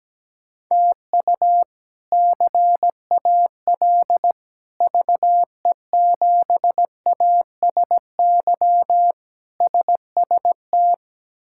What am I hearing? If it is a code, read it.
Morse code, heard as a keyed tone, about 17 wpm, 700 Hz. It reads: TU CAL VE7ASY SST
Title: K1USN SST QSO Practice - 17wpm